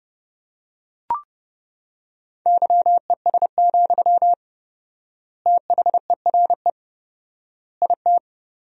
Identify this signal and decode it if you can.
Morse code, heard as a keyed tone, about 32 wpm, 700 Hz. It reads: YES, THERE IT
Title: 4x Word-Spacing - Sentences from Top 500 Words - 30wpm